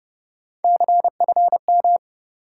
Morse code, heard as a keyed tone, about 33 wpm, 700 Hz. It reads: CFM